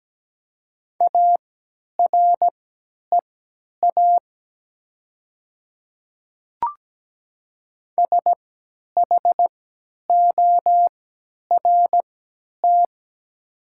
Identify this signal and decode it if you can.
Morse code, heard as a keyed tone, about 17 wpm, spaced as for 10 wpm, 700 Hz. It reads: AREA SHORT
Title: 3x Character Spacing - Top 500 Words - 17wpm